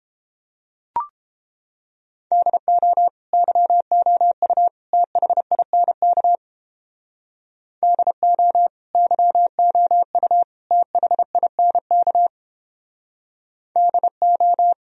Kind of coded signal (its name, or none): Morse code